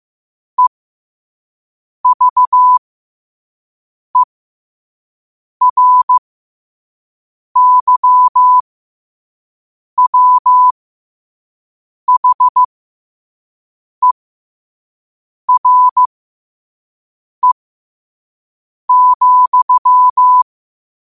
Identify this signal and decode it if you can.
Morse code, heard as a keyed tone, about 15 wpm, spaced as for 5 wpm, 1000 Hz. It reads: EVERYWHERE,
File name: qotdfiles / 5wpm